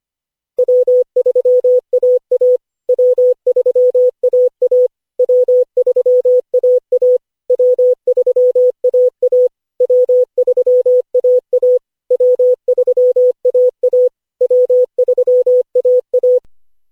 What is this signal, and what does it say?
Morse code, heard as a keyed tone, about 25 wpm, 500 Hz. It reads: W3AA W3AA W3AA W3AA W3AA W3AA W3AA
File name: cw / Session5 cw training